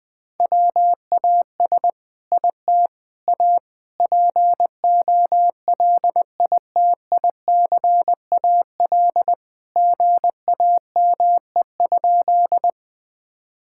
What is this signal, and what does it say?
Morse code, heard as a keyed tone, about 20 wpm, 700 Hz. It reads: WAS IT A POLITICAL GAME?
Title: Sentences from Top 1000 Words - Review Part 5 - 20wpm